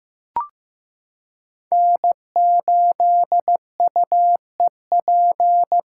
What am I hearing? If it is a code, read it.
Morse code, heard as a keyed tone, about 15 wpm, 700 Hz. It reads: N8UEP
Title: Code Groups - 5 Letter:Numbers - 15wpm